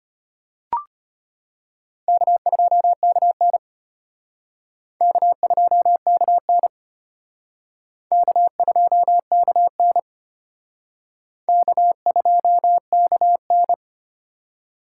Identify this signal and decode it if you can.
Morse code, heard as a keyed tone, about 32 wpm, 700 Hz. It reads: K2KN K2KN K2KN K2KN